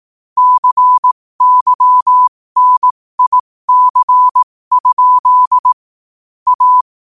Morse code, heard as a keyed tone, about 18 wpm, spaced as for 15 wpm, 1000 Hz. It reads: CYNIC? A
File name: qotdfiles / 15wpm